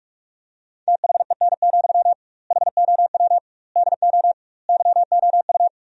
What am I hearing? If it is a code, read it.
Morse code, heard as a keyed tone, about 46 wpm, 700 Hz. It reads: THEN, HOW DO YOU
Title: 2x Word-Spacing - Sentences from Top 100 Words - 45wpm